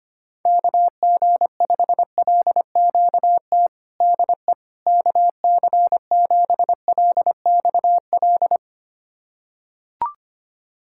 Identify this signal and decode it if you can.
Morse code, heard as a keyed tone, about 25 wpm, 700 Hz. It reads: KG5LQT DE KC7LXL
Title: USA Callsign DE Callsign - 6 Characters - 25wpm